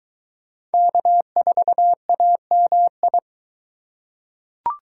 Morse code, heard as a keyed tone, about 23 wpm, 700 Hz. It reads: K4AMI